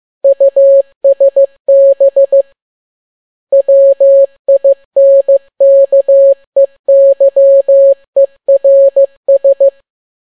Morse code, heard as a keyed tone, about 15 wpm, 550 Hz. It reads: USB WINKEYERS